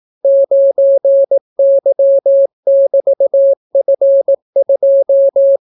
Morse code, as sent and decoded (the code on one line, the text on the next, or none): ----. -.-- -...- ..-. ..---
9Y<BT>F2